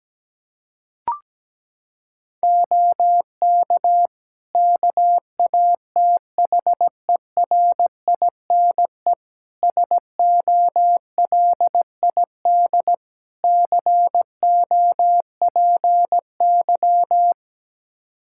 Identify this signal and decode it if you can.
Morse code, heard as a keyed tone, about 17 wpm, 700 Hz. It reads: OK KATHERINE SOLID COPY
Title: Practice QSO - Exchange 3 - More Info - 17wpm